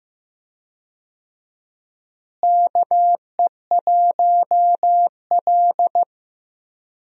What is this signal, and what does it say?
Morse code, heard as a keyed tone, about 15 wpm, 700 Hz. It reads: KE1L
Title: USA Call Signs - 4 Characters - Repeated 2x - 15wpm